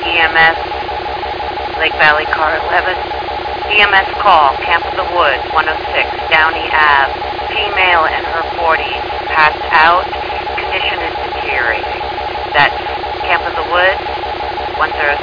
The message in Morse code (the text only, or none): none